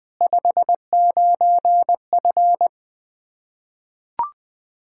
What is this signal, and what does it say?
Morse code, heard as a keyed tone, about 20 wpm, 700 Hz. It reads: 59F
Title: Practice QSO - Exchange 2 - Info - 20wpm